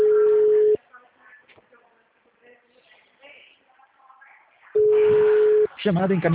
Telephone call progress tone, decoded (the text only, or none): ringback tone